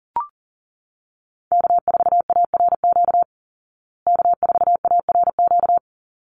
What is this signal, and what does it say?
Morse code, heard as a keyed tone, about 40 wpm, 700 Hz. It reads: K4ARQ K4ARQ
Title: Call Signs Repeated 3x - 40wpm